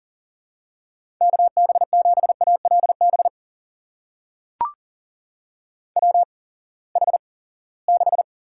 Morse code, heard as a keyed tone, about 39 wpm, spaced as for 22 wpm, 700 Hz. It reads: KB7ALB WH6
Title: USA Call Signs - 6 Characters - 8x Character Spacing - 40wpm